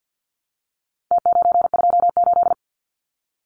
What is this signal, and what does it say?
Morse code, heard as a keyed tone, about 50 wpm, 700 Hz. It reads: T928
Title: Numbers - Contest Style - 4-Digit 50wpm